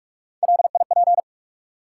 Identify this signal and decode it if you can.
Morse code, heard as a keyed tone, about 47 wpm, 700 Hz. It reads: LIP
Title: Three Letter Words - 45wpm